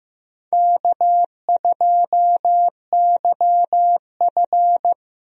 Morse code, heard as a keyed tone, about 15 wpm, 700 Hz. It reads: K2YF
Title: USA Call Signs - 4 Characters - Repeated 2x - 15wpm